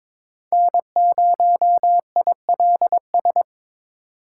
Morse code, heard as a keyed tone, about 22 wpm, 700 Hz. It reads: N0ILS
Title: USA Call Signs - 4x Character Spacing - 22wpm